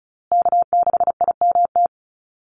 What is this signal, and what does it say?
Morse code, heard as a keyed tone, about 34 wpm, 700 Hz. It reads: K6IMT
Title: USA Call Signs Repeated 3x 35wpm